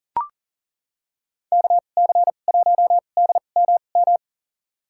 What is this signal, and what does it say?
Morse code, heard as a keyed tone, about 40 wpm, spaced as for 29 wpm, 700 Hz. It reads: KC1DMM
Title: USA Call Signs - 6 Characters - 2x Character Spacing - 40wpm